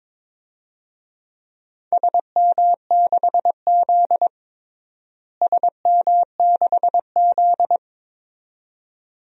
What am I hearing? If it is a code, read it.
Morse code, heard as a keyed tone, about 22 wpm, 700 Hz. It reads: SM6Z SM6Z
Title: DXCC-100 Call Signs - 4 Characters - Repeated 2x - 22wpm